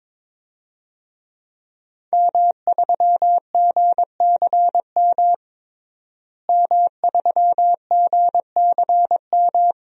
Morse code, heard as a keyed tone, about 22 wpm, 700 Hz. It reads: M3GCM M3GCM
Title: DXCC-100 Call Signs - 5 Characters - Repeated 2x - 22wpm